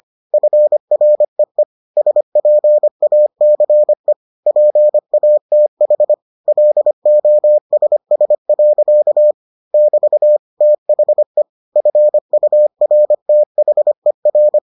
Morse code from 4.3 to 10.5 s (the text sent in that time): PATH LOSS. <BT>